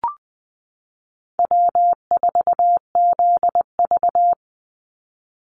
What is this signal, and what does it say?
Morse code, heard as a keyed tone, about 20 wpm, 700 Hz. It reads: W4ZV
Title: NA SOTA Chaser Callsigns 20wpm